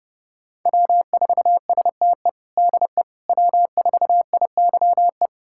Morse code, heard as a keyed tone, about 30 wpm, 700 Hz. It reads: W4STE DE W4IYE
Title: USA Callsign DE Callsign - 5 Characters - 30wpm